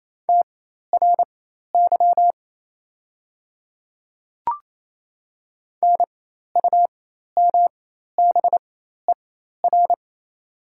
Morse code, heard as a keyed tone, about 28 wpm, spaced as for 13 wpm, 690 Hz. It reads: TRY NUMBER
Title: 4x Character Spacing - Top 200 Words - 28wpm